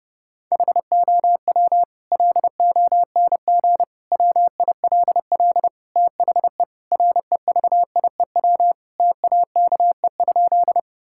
Morse code, heard as a keyed tone, about 32 wpm, 700 Hz. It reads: HOW LONG WILL THE REVIEW TAKE?
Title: Sentences from Top 1000 Words - Review Part 6 - 30wpm